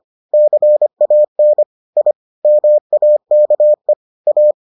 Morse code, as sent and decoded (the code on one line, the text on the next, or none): -.-. .- -. / .. / -- .- -.- . / .-
CAN I MAKE A